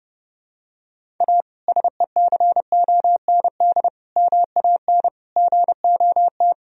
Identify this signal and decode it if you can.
Morse code, heard as a keyed tone, about 32 wpm, 700 Hz. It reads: A SECOND MAN GOT
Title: Sentences from Top 1000 Words - Review Part 1 - 30wpm